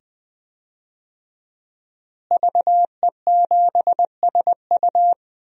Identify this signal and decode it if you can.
Morse code, heard as a keyed tone, about 20 wpm, 700 Hz. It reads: VE7SU
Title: DXCC-100 Call Signs - 5 Characters - Repeated 2x - 20wpm